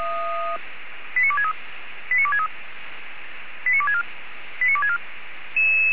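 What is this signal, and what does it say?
Five-tone selective call (ZVEI), heard as quick runs of five tones, tones of about 70 ms. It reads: 89263 89263 89264 89264
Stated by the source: ZVEI five-tone selective call received by SDR.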